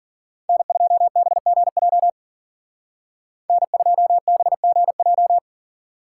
Morse code, heard as a keyed tone, about 51 wpm, 700 Hz. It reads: N2BGJ N2BGJ